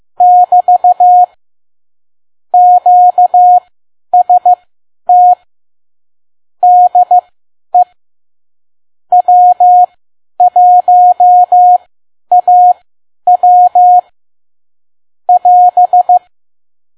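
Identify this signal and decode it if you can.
Morse code, heard as a keyed tone, about 15 wpm, spaced as for 10 wpm, 720 Hz. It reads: <BT> QST DE W1AW <AS>
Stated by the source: Morse code practice at 10 WPM.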